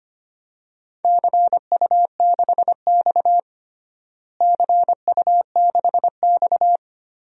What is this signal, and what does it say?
Morse code, heard as a keyed tone, about 25 wpm, 700 Hz. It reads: CU6X CU6X
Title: DXCC-100 Call Signs - 4 Characters - Repeated 3x - 25wpm